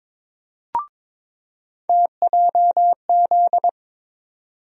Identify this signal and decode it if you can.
Morse code, heard as a keyed tone, about 21 wpm, 700 Hz. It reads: TJZ
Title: USA Call Sign Postfixes - 3 Letters 22wpm